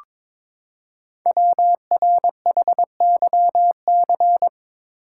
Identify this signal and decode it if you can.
Morse code, heard as a keyed tone, about 23 wpm, 700 Hz. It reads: WRHYC
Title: Code Groups - 5 Letters - 22wpm